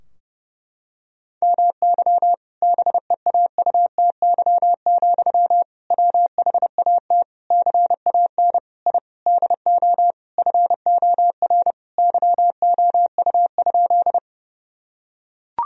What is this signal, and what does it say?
Morse code, heard as a keyed tone, about 30 wpm, 700 Hz. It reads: MY BEAUTY, WHAT CAN I DO FOR YOU?
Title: Sentences from Top 1000 Words - Review Part 2 - 30wpm